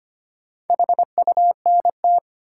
Morse code, heard as a keyed tone, about 25 wpm, 700 Hz. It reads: HUNT
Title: Sentences from Top 1600 Words - New Word Drill - 25wpm